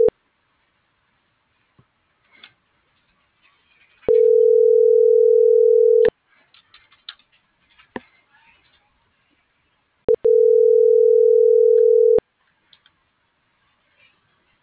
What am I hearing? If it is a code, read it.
Ringback tone (North America), 440 + 480 Hz.